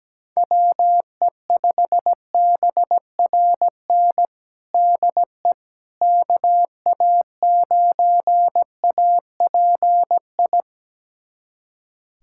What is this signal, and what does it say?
Morse code, heard as a keyed tone, about 17 wpm, 700 Hz. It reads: WE5BRN DE KA9API
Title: USA Callsign DE Callsign - 6 Characters - 17wpm